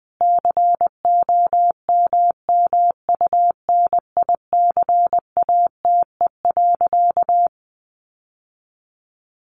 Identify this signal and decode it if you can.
Morse code, heard as a keyed tone, about 20 wpm, 700 Hz. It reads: COMMUNICATE.
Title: Sentences from Top 1300 Words 20wpm